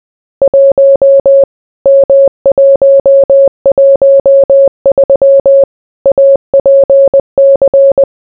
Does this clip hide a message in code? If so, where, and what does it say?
Yes: Morse code from 0.4 to 8.0 s, heard as a keyed tone, about 20 wpm, 550 Hz. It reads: 1 M113 APC